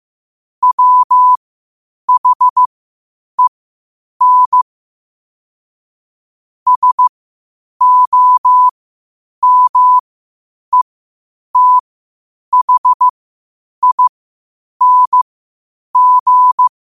Morse code, heard as a keyed tone, about 15 wpm, spaced as for 8 wpm, 1000 Hz. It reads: WHEN SOMETHING
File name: qotdfiles / 8wpm